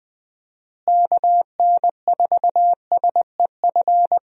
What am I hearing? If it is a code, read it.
Morse code, heard as a keyed tone, about 20 wpm, 700 Hz. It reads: KN4SEF
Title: USA Call Signs - 6 Characters - Repeated 2x - 20wpm